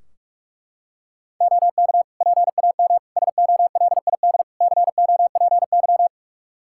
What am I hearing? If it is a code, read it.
Morse code, heard as a keyed tone, about 41 wpm, 700 Hz. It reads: OK PAM SOLID COPY
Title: K1USN SST QSO Practice 45wpm